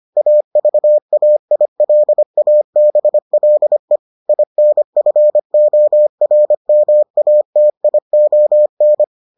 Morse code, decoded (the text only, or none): AVAILABLE INFORMATION